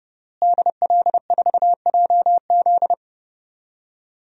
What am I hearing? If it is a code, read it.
Morse code, heard as a keyed tone, about 30 wpm, 700 Hz. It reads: DL4JZ
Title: DXCC-100 Call Signs - 5 Characters - Repeated 3x - 30wpm